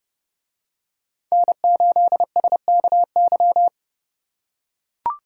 Morse code, heard as a keyed tone, about 29 wpm, 700 Hz. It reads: N8SKY